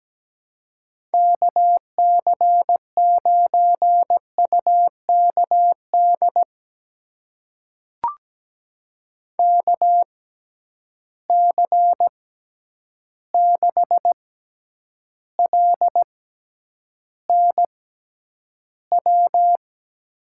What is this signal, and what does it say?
Morse code, heard as a keyed tone, about 17 wpm, spaced as for 9 wpm, 700 Hz. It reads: KC9UKD KC6LNW
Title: USA Call Signs - 6 Characters - 6x Character Spacing - 17wpm